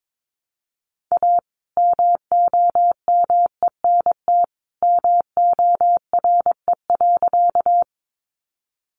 Morse code, heard as a keyed tone, about 22 wpm, 700 Hz. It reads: A MOMENT MORE.
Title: Sentences from Top 1000 Words 22wpm